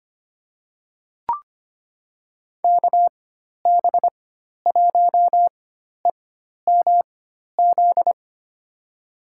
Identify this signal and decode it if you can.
Morse code, heard as a keyed tone, about 25 wpm, spaced as for 12 wpm, 700 Hz. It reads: KB1EMZ
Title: USA Call Signs - 6 Characters - 4x Character Spacing - 25wpm